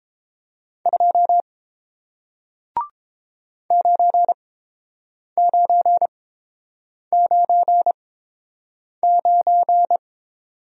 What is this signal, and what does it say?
Morse code, heard as a keyed tone, about 29 wpm, spaced as for 7 wpm, 700 Hz. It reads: 2 9999